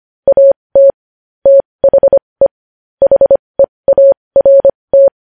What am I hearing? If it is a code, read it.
Morse code, heard as a keyed tone, about 25 wpm, spaced as for 20 wpm, 550 Hz. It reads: AT THE HEART
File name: podcast / News headlines-1x-25x20